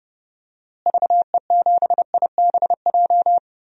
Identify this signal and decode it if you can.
Morse code, heard as a keyed tone, about 32 wpm, 700 Hz. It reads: VE7IBJ
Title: DXCC-100 Call Signs - 6 Characters - Repeated 3x - 30wpm